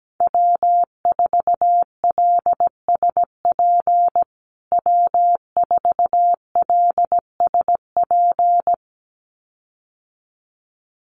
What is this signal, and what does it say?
Morse code, heard as a keyed tone, about 17 wpm, 700 Hz. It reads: W4LSP W4LSP
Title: USA Call Signs Repeated 2x 17wpm